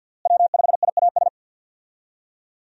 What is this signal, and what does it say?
Morse code, heard as a keyed tone, about 51 wpm, 700 Hz. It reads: W5IAS
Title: USA Call Signs - 5 Characters - Repeated 3x - 50wpm